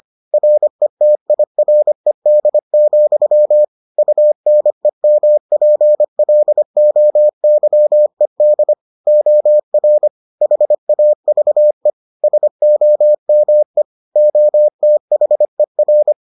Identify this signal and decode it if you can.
Morse code, heard as a keyed tone, about 25 wpm, 600 Hz. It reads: RETIRED, UNEMPLOYED OR HAVE SOME OTHER